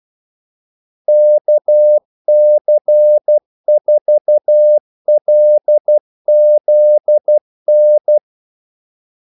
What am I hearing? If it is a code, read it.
Morse code, heard as a keyed tone, about 12 wpm, 600 Hz. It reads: KC4LZN
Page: 2 ekstra ordmellemrum mellem kaldesignalerne